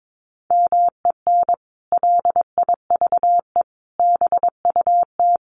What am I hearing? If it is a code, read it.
Morse code, heard as a keyed tone, about 22 wpm, 700 Hz. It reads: MEN LIVE BUT
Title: Sentences from Top 1400 Words 22wpm